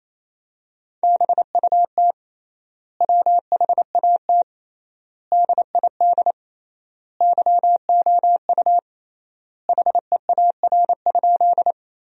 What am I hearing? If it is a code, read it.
Morse code, heard as a keyed tone, about 28 wpm, 700 Hz. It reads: BUT WHAT DID YOU HEAR?
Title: 3x Word-Spacing - Sentences from Top 300 Words - 28wpm